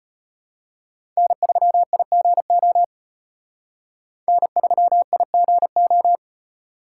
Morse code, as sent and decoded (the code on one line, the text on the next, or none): -. ...-- .. --. --- / -. ...-- .. --. ---
N3IGO N3IGO